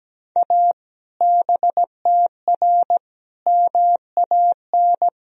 Morse code, heard as a keyed tone, about 17 wpm, 700 Hz. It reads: A BTR MAN
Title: Sentences from Top 400 Words and Abbreviations 17wpm